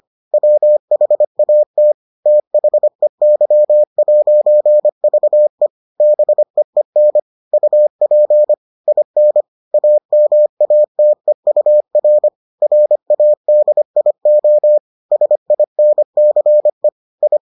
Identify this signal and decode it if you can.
Morse code, heard as a keyed tone, about 25 wpm, 600 Hz. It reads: WHAT THEY'VE BEEN UP IN AMATEUR RADIO SINCE I